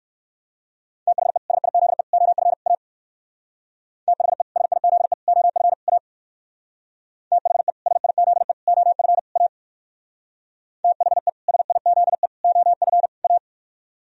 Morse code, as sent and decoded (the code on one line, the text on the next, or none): - .... / ... .. --.. / --- ..-. / .- / - .... / ... .. --.. / --- ..-. / .- / - .... / ... .. --.. / --- ..-. / .- / - .... / ... .. --.. / --- ..-. / .-
THE SIZE OF A THE SIZE OF A THE SIZE OF A THE SIZE OF A